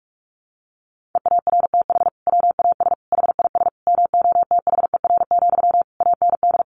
Morse code, heard as a keyed tone, about 46 wpm, 700 Hz. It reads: EARTH WAS HIS MOTHER, AND
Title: Sentences from Top 1200 Words 45wpm